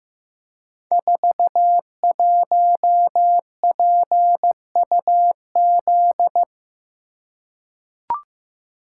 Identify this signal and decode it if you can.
Morse code, heard as a keyed tone, about 15 wpm, 700 Hz. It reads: 41PUZ
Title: Code Groups - 5 Letter:Numbers - 15wpm